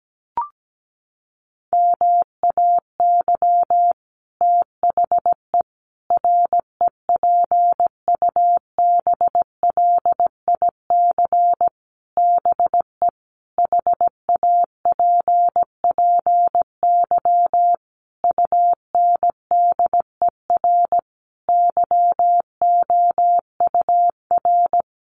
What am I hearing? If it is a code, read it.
Morse code, heard as a keyed tone, about 17 wpm, 700 Hz. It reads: MAY THE REPUBLIC BE HAPPY UNDER YOUR
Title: Sentences from Top 1500 Words 17wpm